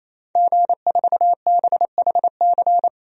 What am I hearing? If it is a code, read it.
Morse code, heard as a keyed tone, about 28 wpm, 700 Hz. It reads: G4BHC
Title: DXCC-100 Call Signs - 5 Characters - 28wpm